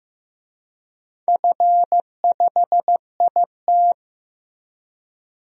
Morse code, heard as a keyed tone, about 15 wpm, 700 Hz. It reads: F5IT
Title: DXCC-100 Call Signs - 4 Characters - Repeated 2x - 15wpm